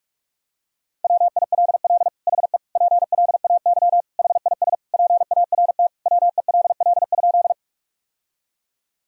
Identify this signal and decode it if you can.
Morse code, heard as a keyed tone, about 42 wpm, 700 Hz. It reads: WILL HE PLAY HIS PART WELL?
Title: Sentences from Top 1000 Words - Review Part 1 - 45wpm